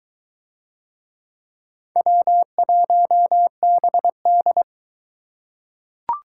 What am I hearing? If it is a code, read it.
Morse code, heard as a keyed tone, about 23 wpm, 700 Hz. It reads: W1BD